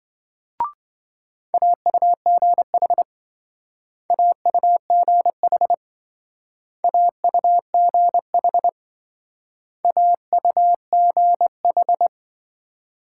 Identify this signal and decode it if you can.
Morse code, heard as a keyed tone, about 25 wpm, 700 Hz. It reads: AUGH AUGH AUGH AUGH